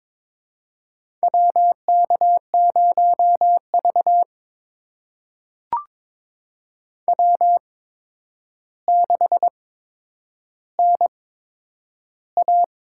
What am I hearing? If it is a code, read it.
Morse code, heard as a keyed tone, about 22 wpm, spaced as for 9 wpm, 700 Hz. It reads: WK0V W6NA
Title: USA Call Signs - 4 Characters - 8x Character Spacing - 22wpm